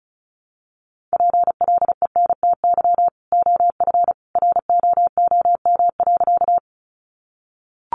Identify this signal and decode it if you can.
Morse code, heard as a keyed tone, about 34 wpm, 700 Hz. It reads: PLENTY OF ROOM.
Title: Sentences from Top 1300 Words 35wpm